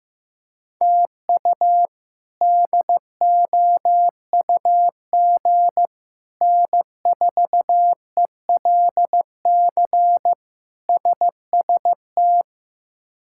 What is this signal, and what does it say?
Morse code, heard as a keyed tone, about 16 wpm, 700 Hz. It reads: TU DOUG N4ELC SST
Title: K1USN SST QSO Practice - 15wpm